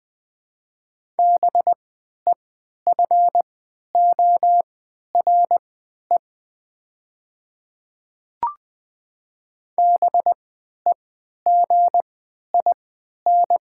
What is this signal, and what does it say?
Morse code, heard as a keyed tone, about 20 wpm, spaced as for 11 wpm, 700 Hz. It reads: BEFORE BEGIN
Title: 3x Character Spacing - Top 300 Words - 20wpm